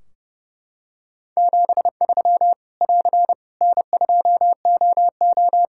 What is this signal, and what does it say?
Morse code, heard as a keyed tone, about 30 wpm, 700 Hz. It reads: 73 <AR> N2OO
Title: Practice QSO - Exchange 4 - Ending - 30wpm